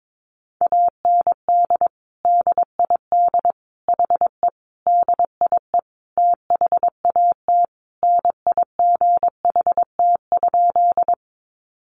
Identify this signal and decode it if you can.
Morse code, heard as a keyed tone, about 22 wpm, 700 Hz. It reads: AND DID HE DIE THAT NIGHT?
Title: Sentences from Top 1300 Words 22wpm